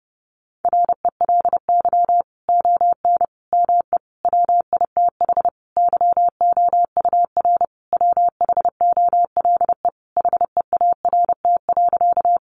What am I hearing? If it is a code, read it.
Morse code, heard as a keyed tone, about 30 wpm, 700 Hz. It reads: RELY ON ME WITH YOUR WHOLE HEART.
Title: Sentences from Top 1200 Words 30wpm